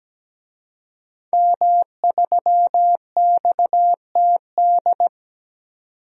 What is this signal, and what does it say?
Morse code, heard as a keyed tone, about 17 wpm, 700 Hz. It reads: M3XTD
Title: DXCC-100 Call Signs - 5 Characters - 17wpm